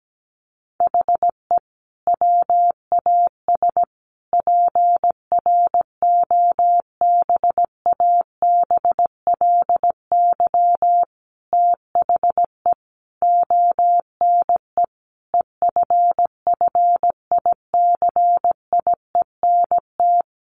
Morse code, heard as a keyed tone, about 17 wpm, 700 Hz. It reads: HE WAS PROBABLY THE ONE EFFICIENT
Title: Sentences from Top 1300 Words 17wpm